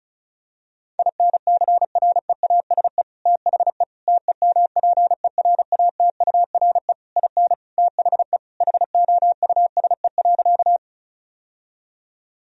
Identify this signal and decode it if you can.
Morse code, heard as a keyed tone, about 36 wpm, 700 Hz. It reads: INCREASE THE TEMPERATURE IN THE HOUSE.
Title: Sentences from Top 1000 Words - Review Part 4 - 35wpm